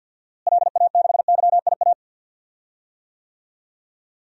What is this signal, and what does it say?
Morse code, heard as a keyed tone, about 52 wpm, 700 Hz. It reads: LA6YIA
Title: DXCC-100 Call Signs - 6 Characters - Repeated 3x - 50wpm